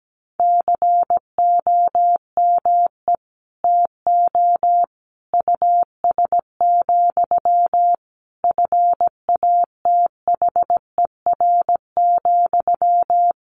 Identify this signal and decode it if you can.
Morse code, heard as a keyed tone, about 17 wpm, 700 Hz. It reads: COME TO US, FATHER,
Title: Sentences from Top 1300 Words 17wpm